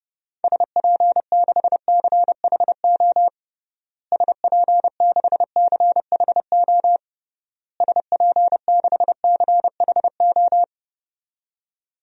Morse code, heard as a keyed tone, about 30 wpm, 700 Hz. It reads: SP6CHO SP6CHO SP6CHO
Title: DXCC-100 Call Signs - 6 Characters - Repeated 3x - 30wpm